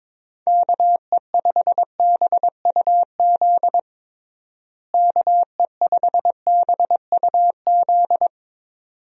Morse code, heard as a keyed tone, about 22 wpm, 700 Hz. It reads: KE5BUZ KE5BUZ
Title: USA Call Signs - 6 Characters - Repeated 2x - 22wpm